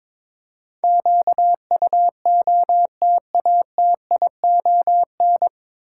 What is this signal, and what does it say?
Morse code, heard as a keyed tone, about 22 wpm, 700 Hz. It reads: QUOTATION
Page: Morse Code Ninja practice 22wpm